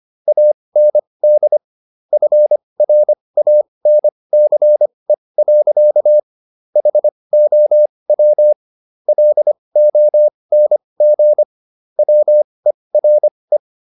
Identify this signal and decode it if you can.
Morse code, heard as a keyed tone, about 25 wpm, spaced as for 20 wpm, 600 Hz. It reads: AND FRANCE. HOW LONG WERE